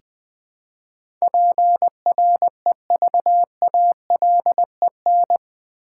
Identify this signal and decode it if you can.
Morse code, heard as a keyed tone, about 21 wpm, 700 Hz. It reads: PREVALEN
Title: Build A Word - 4 Letters At A Time - 10-Letter Words - 20wpm